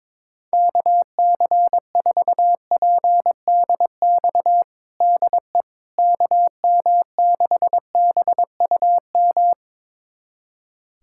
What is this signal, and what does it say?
Morse code, heard as a keyed tone, about 21 wpm, 700 Hz. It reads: KC4PDX DE KM6BUM
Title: USA Callsign DE Callsign 22wpm